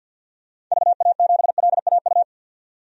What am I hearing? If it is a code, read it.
Morse code, heard as a keyed tone, about 52 wpm, 700 Hz. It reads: VA7LAU
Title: DXCC-100 Call Signs - 6 Characters - 50wpm